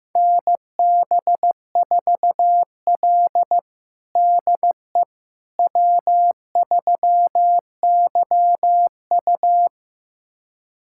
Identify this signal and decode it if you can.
Morse code, heard as a keyed tone, about 16 wpm, 700 Hz. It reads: NB4L DE W3YU
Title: USA Callsign DE Callsign - 4 Characters - 15wpm